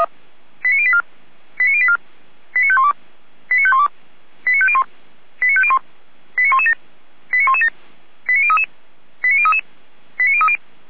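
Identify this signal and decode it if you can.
Five-tone selective call (ZVEI), heard as quick runs of five tones, tones of about 70 ms. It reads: 89084 89084 89512 89512 89501 89501 89117 89117 89033 89033 89030
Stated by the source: Received by SDR, ZVEI five-tone.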